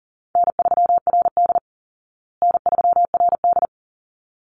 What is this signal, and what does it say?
Morse code, heard as a keyed tone, about 40 wpm, 700 Hz. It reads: N3RD N3RD
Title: Easy Call Signs Repeated 2x - 40wpm